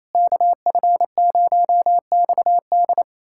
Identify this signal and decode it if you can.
Morse code, heard as a keyed tone, about 28 wpm, 700 Hz. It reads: KF0XD
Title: USA Callsign DE Callsign - 5 Characters - 28wpm